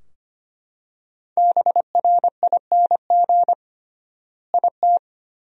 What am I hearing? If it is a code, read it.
Morse code, heard as a keyed tone, about 25 wpm, 700 Hz. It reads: BRING IT
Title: 3x Word-Spacing - Sentences from Top 500 Words - 25wpm